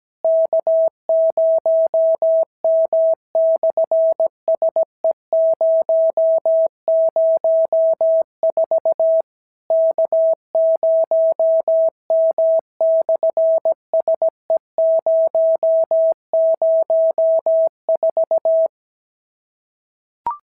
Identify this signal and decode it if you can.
Morse code, heard as a keyed tone, about 17 wpm, 650 Hz. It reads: K0M/SE004 K0M/SE004
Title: SOTA QSO Practice 17wpm